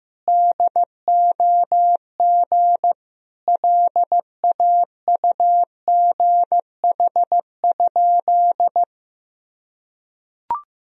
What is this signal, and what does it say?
Morse code, heard as a keyed tone, about 15 wpm, 700 Hz. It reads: DOG LAUGH?
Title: Sentences from Top 500 Words and Abbreviations 15wpm